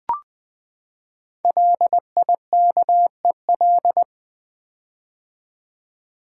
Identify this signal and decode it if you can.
Morse code, heard as a keyed tone, about 20 wpm, 700 Hz. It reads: LIKEL
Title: Build A Word - 5 Letters At A Time - 10-Letter Words - 20wpm